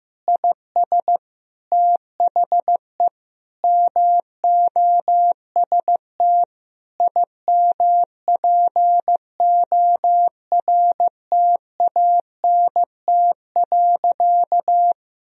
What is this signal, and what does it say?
Morse code, heard as a keyed tone, about 16 wpm, 700 Hz. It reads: IS THE MOST IMPORTANT.
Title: Sentences from Top 1000 Words - Review Part 2 - 15wpm